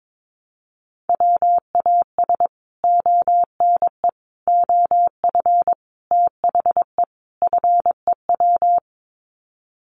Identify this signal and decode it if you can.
Morse code, heard as a keyed tone, about 22 wpm, 700 Hz. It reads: WAS ONE OF THE FEW
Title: Sets of 5 Words Encore 22wpm